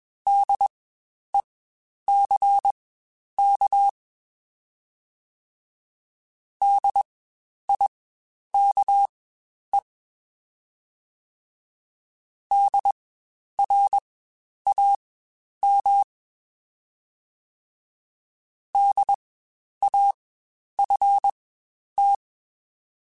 Morse code, heard as a keyed tone, about 22 wpm, spaced as for 10 wpm, 800 Hz. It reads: DECK DIKE DRAM DAFT
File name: Spaced Alphas / Spaced Alphas mp3s / Ds sp @20